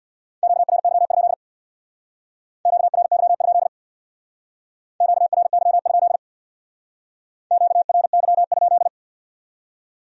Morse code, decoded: QRY? QRY? QRY? QRY?

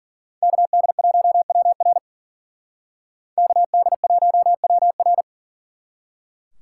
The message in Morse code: KD1WR KD1WR

